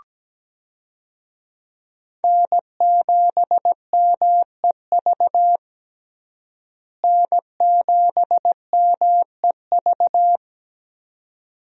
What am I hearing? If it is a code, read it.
Morse code, heard as a keyed tone, about 17 wpm, 700 Hz. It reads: N7MEV N7MEV
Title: USA Call Signs - 5 Characters - Repeated 2x - 17wpm